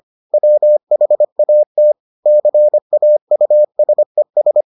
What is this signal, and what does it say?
Morse code, heard as a keyed tone, about 25 wpm, 600 Hz. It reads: WHAT CAUSES